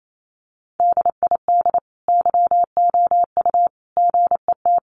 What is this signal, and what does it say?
Morse code, heard as a keyed tone, about 28 wpm, 700 Hz. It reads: DID YOU GET
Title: Sentences from Top 1400 Words 28wpm